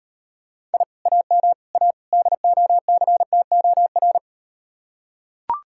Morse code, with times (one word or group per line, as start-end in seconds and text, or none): 0.7-0.8 I
1.1-1.5 AM
1.7-1.9 A
2.1-4.2 DOCTOR